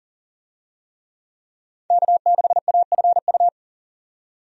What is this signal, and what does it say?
Morse code, heard as a keyed tone, about 40 wpm, 700 Hz. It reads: K6AFU
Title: USA Call Signs - 5 Characters - 40wpm